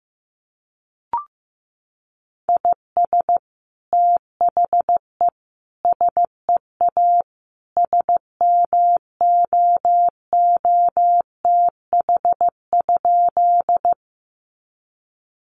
Morse code, heard as a keyed tone, about 15 wpm, 700 Hz. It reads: IS THE SEA SMOOTH?
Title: Sentences from Top 1400 Words 15wpm